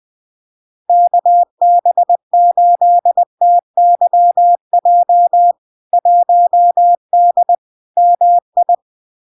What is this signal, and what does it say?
Morse code, heard as a keyed tone, about 20 wpm, 700 Hz. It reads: KB8TYJ 1D MI
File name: LARK-FD-2022-Log-in-CW-20wpm-700-Hz